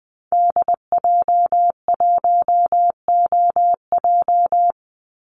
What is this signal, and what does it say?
Morse code, heard as a keyed tone, about 20 wpm, 700 Hz. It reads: DJ1OJ
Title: Call Signs Repeated 2x - 20wpm